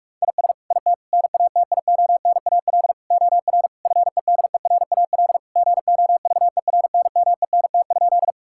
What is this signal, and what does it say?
Morse code, heard as a keyed tone, about 46 wpm, 700 Hz. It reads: IS IT NATIONAL OR FEDERAL GOVERNMENT?
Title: Sentences from Top 1000 Words - Review Part 5 - 45wpm